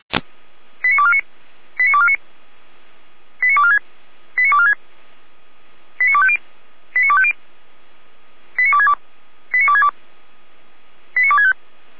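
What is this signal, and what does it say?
Five-tone selective call (ZVEI), heard as quick runs of five tones, tones of about 70 ms. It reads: 89249 89249 89246 89246 89250 89250 89262 89262 89265
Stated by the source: ZVEI five-tone selective call 89187 received by SDR.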